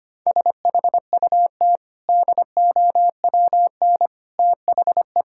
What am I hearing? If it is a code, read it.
Morse code, heard as a keyed tone, about 25 wpm, 700 Hz. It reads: SHUT DOWN THE
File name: Easy Sentences with Hints - 25wpm